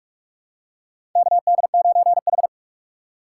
Morse code, heard as a keyed tone, about 45 wpm, 690 Hz. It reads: KD9H
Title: USA Call Signs - 3x Character Spacing - 45wpm